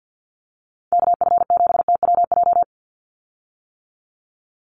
Morse code, heard as a keyed tone, about 50 wpm, 700 Hz. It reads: KF7TWJ
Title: USA Call Signs 50wpm